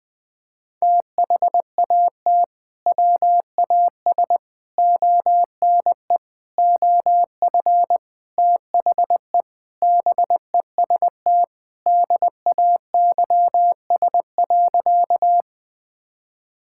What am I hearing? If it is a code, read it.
Morse code, heard as a keyed tone, about 21 wpm, 700 Hz. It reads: THAT WAS ONE OF THE BEST DAYS.
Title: Sentences with 5-Word Phrases - 20wpm